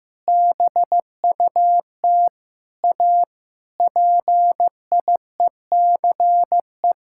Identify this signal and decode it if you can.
Morse code, heard as a keyed tone, about 15 wpm, 700 Hz. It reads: BUT A PIECE
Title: Sentences from Top 1600 Words 15wpm